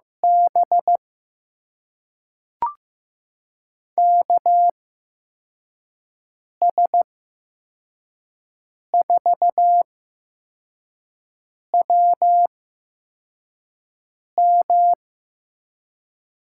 Morse code, heard as a keyed tone, about 15 wpm, spaced as for 4 wpm, 700 Hz. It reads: B KS4WM
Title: USA Call Signs - 8x Character Spacing - 15wpm